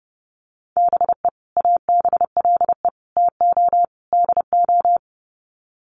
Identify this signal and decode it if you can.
Morse code, heard as a keyed tone, about 30 wpm, 700 Hz. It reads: BE ABLE TO DO
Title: Sets of 4 Words 30wpm